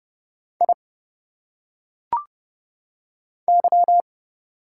Morse code, heard as a keyed tone, about 30 wpm, 700 Hz. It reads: I Y
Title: Single Letters - Phonetic Pronunciation 30wpm